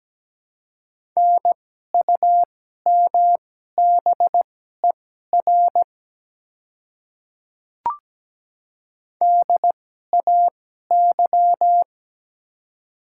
Morse code, heard as a keyed tone, about 17 wpm, spaced as for 12 wpm, 700 Hz. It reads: NUMBER DAY
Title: 2x Character Spacing - Top 100 Words - 17wpm